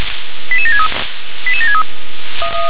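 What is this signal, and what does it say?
Five-tone selective call (ZVEI), heard as quick runs of five tones, tones of about 70 ms. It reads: 88763 88763
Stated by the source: ZVEI five-tone selective call received on an SDR.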